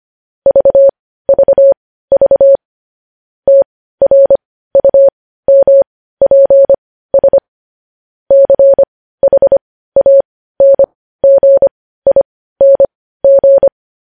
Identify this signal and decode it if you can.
Morse code, heard as a keyed tone, about 25 wpm, spaced as for 15 wpm, 550 Hz. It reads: VVV TRUMPS CHANGING
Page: news headlines in morse code at 15 wpm